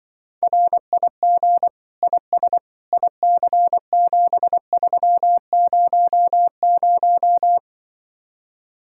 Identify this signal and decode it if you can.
Morse code, heard as a keyed tone, about 24 wpm, 700 Hz. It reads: RIG IS IC7300
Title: Speed-Racing Common QSO Phrases - 20wpm